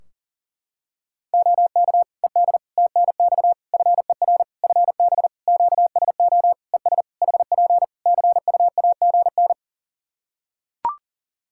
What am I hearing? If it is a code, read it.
Morse code, heard as a keyed tone, about 37 wpm, 700 Hz. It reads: OK ED TNX FER FB QSO ES HP CUAGN
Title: K1USN SST QSO Practice 40wpm